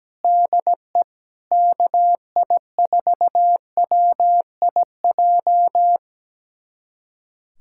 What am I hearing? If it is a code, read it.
Morse code, heard as a keyed tone, about 17 wpm, 700 Hz. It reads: DE KI4WIJ
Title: USA Callsign DE Callsign - 6 Characters - 17wpm